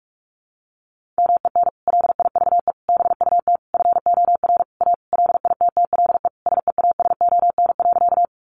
Morse code, heard as a keyed tone, about 44 wpm, 700 Hz. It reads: MEN LIVE BUT FOR A LITTLE SEASON.
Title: Sentences from Top 1400 Words 45wpm